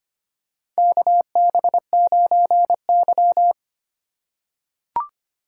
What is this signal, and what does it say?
Morse code, heard as a keyed tone, about 25 wpm, 700 Hz. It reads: KB9Y
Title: USA Call Signs - 2x Character Spacing - 25wpm